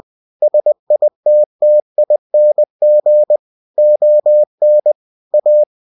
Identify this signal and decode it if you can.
Morse code, heard as a keyed tone, about 20 wpm, 600 Hz. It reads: SITTING ON A
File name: cw-books / winnie-the-pooh / winnie-20wpm-page-0084